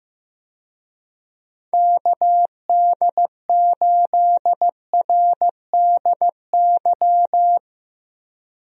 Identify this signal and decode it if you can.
Morse code, heard as a keyed tone, about 15 wpm, 700 Hz. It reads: KD8RDY
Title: USA Call Signs - 6 Characters - 15wpm